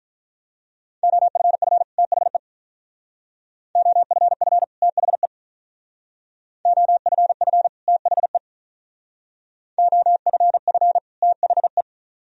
Morse code, heard as a keyed tone, about 44 wpm, 700 Hz. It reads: OFF THE OFF THE OFF THE OFF THE